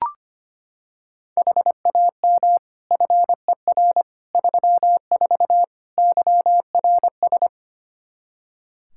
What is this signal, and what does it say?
Morse code, heard as a keyed tone, about 25 wpm, 700 Hz. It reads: HAM FER 34 YRS
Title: Practice QSO - Exchange 3 - More Info - 25wpm